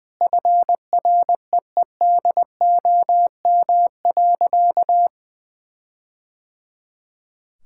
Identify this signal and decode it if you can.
Morse code, heard as a keyed tone, about 20 wpm, 700 Hz. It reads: FREEDOM.
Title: Sentences from Top 1600 Words 20wpm